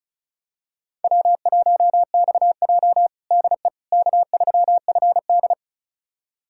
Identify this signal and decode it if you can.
Morse code, heard as a keyed tone, about 36 wpm, 700 Hz. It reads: W1XJ DE K3FD
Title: USA Callsign DE Callsign - 4 Characters - 35wpm